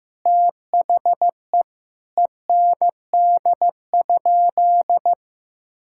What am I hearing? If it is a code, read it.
Morse code, heard as a keyed tone, about 15 wpm, 700 Hz. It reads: THE END?
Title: Sentences from Top 1000 Words - Review Part 5 - 15wpm